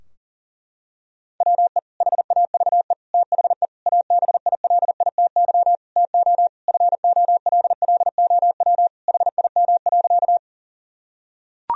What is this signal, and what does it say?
Morse code, heard as a keyed tone, about 37 wpm, 700 Hz. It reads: WE HAVE THE ABILITY TO FOLLOW HIM.
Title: Sentences from Top 1000 Words - Review Part 3 - 40wpm